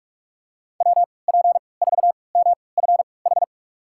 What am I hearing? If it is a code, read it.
Morse code, heard as a keyed tone, about 43 wpm, spaced as for 25 wpm, 700 Hz. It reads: WP4MFH
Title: USA Call Signs - 3x Character Spacing - 45wpm